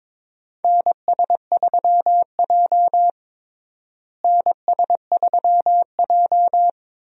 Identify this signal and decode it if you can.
Morse code, heard as a keyed tone, about 22 wpm, 700 Hz. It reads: NS3J NS3J
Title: USA Call Signs - 4 Characters - Repeated 3x - 22wpm